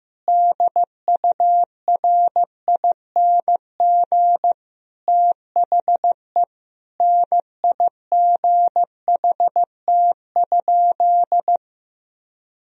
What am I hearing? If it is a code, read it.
Morse code, heard as a keyed tone, about 15 wpm, 700 Hz. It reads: DURING THE NIGHT?
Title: Sentences from Top 1000 Words - Review Part 2 - 15wpm